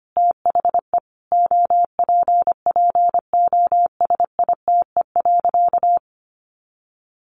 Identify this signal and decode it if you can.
Morse code, heard as a keyed tone, about 25 wpm, 700 Hz. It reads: THE OPPOSITE.
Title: Sentences from Top 1300 Words 25wpm